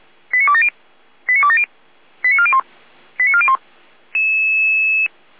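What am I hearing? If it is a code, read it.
Five-tone selective call (ZVEI), heard as quick runs of five tones, tones of about 70 ms. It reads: 89270 89270 89401 89401
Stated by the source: ZVEI five-tone selective call received by SDR, Feuerwehr Orlamünde.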